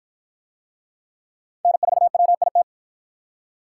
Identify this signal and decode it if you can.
Morse code, heard as a keyed tone, about 47 wpm, 700 Hz. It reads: N4WIT